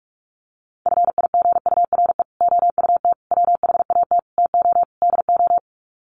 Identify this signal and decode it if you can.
Morse code, heard as a keyed tone, about 44 wpm, 690 Hz. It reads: FIGURE OUT WHAT TO DO
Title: Sets of 5 Words Encore2 45wpm